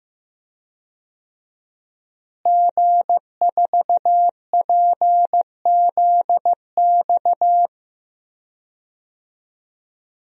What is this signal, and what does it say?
Morse code, heard as a keyed tone, about 15 wpm, 700 Hz. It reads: G4PZX
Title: DXCC-100 Call Signs - 5 Characters - Repeated 3x - 15wpm